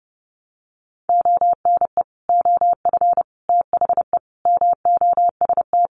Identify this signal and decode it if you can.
Morse code, heard as a keyed tone, about 30 wpm, 700 Hz. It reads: ONE OF THE MOST
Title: Sets of 5 Words Encore2 30wpm